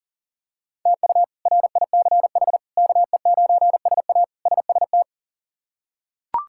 Morse code, heard as a keyed tone, about 37 wpm, 700 Hz. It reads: TU RICH KE9SA SST
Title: K1USN SST QSO Practice - 40wpm